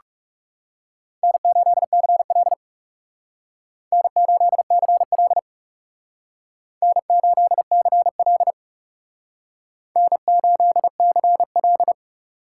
Morse code, heard as a keyed tone, about 39 wpm, 700 Hz. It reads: N8CL N8CL N8CL N8CL